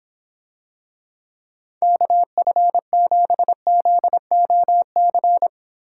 Morse code, heard as a keyed tone, about 26 wpm, 700 Hz. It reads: KF7ZOC